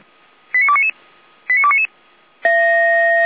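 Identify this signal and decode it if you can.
Five-tone selective call (ZVEI), heard as quick runs of five tones, tones of about 70 ms. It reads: 89299 89299
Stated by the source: Five-tone ZVEI selective call 89300, received on an SDR.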